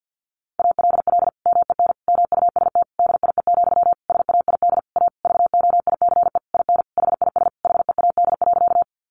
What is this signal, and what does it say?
Morse code, heard as a keyed tone, about 50 wpm, 700 Hz. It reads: ALL MEN MUST DIE, SAID A VOICE IN HIS HEAD.
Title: Sentences from Top 1300 Words 50wpm